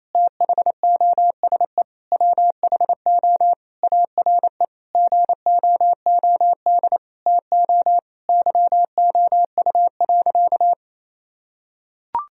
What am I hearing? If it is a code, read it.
Morse code, heard as a keyed tone, about 27 wpm, 700 Hz. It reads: THOSE WHO ARE GOOD TO YOU.
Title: Sentences from Top 1000 Words - Review Part 1 - 28wpm